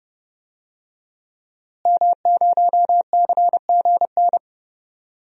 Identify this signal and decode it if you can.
Morse code, heard as a keyed tone, about 30 wpm, 700 Hz. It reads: M0CGN
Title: DXCC-100 Call Signs - 5 Characters - Repeated 3x - 30wpm